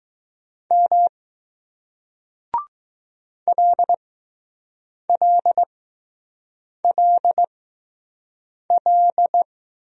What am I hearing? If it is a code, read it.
Morse code, heard as a keyed tone, about 20 wpm, spaced as for 6 wpm, 700 Hz. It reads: M LLLL